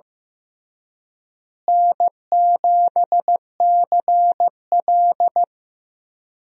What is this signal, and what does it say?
Morse code, heard as a keyed tone, about 15 wpm, 700 Hz. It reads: N7CL
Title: USA Call Signs - 4 Characters - Repeated 3x - 15wpm